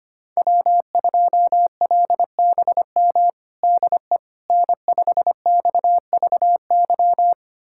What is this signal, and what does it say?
Morse code, heard as a keyed tone, about 25 wpm, 700 Hz. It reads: W2LBM DE N5XVY
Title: USA Callsign DE Callsign 25wpm